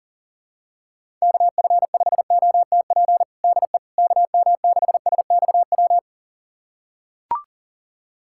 Morse code, heard as a keyed tone, about 37 wpm, 700 Hz. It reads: KF5OTP DE KM6SXW
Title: USA Callsign DE Callsign 40wpm